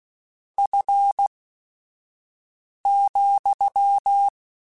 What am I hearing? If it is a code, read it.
Morse code, heard as a keyed tone, about 16 wpm, spaced as for 5 wpm, 800 Hz. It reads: F,
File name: Spaced Characters / Spaced Characters-mp3s / Weather5 1 sp.@5